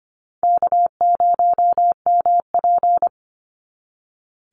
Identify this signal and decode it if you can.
Morse code, heard as a keyed tone, about 25 wpm, 700 Hz. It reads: K0MP
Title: NA SOTA Chaser Callsigns 25wpm